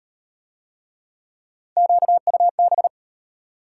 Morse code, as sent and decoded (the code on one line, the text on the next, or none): --.- ..- -...
QUB